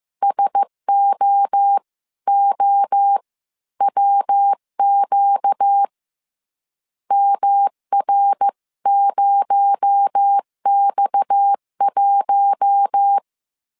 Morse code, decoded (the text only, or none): SOOWQ MR0X1